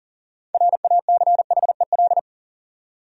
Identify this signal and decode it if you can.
Morse code, heard as a keyed tone, about 40 wpm, 700 Hz. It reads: RACHEL
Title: USA Common Names - 40wpm